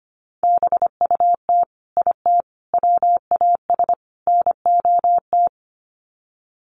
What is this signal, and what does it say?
Morse code, heard as a keyed tone, about 25 wpm, 700 Hz. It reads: BUT IT WAS NOT
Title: Sets of 4 Words 25wpm